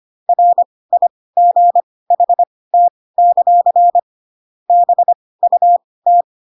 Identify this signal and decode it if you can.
Morse code, heard as a keyed tone, about 25 wpm, spaced as for 18 wpm, 700 Hz. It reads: RIGHT; BUT
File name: Pinocchio / Pinocchio ch2 12 25 ik1hge